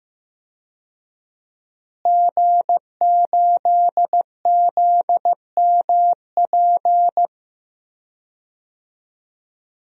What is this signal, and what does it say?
Morse code, heard as a keyed tone, about 15 wpm, 700 Hz. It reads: G8ZMP
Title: DXCC-100 Call Signs - 5 Characters - Repeated 3x - 15wpm